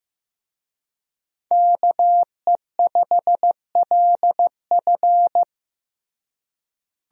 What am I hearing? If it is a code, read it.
Morse code, heard as a keyed tone, about 15 wpm, 700 Hz. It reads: KE5LF
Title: USA Call Signs - 5 Characters - Repeated 2x - 15wpm